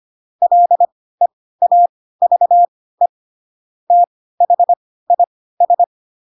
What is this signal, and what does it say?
Morse code, heard as a keyed tone, about 25 wpm, spaced as for 16 wpm, 700 Hz. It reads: LEAVE THIS
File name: Pinocchio / Pinocchio ch4 10 25 ik1hge